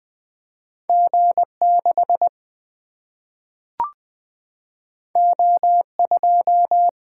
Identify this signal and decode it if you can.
Morse code, heard as a keyed tone, about 20 wpm, 700 Hz. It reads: G 6 O 2
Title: Single-Letter Single-Number 20wpm